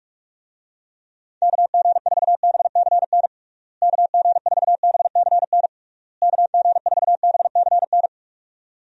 Morse code, heard as a keyed tone, about 46 wpm, 700 Hz. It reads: KG4BCN KG4BCN KG4BCN
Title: USA Call Signs - 6 Characters - Repeated 3x - 45wpm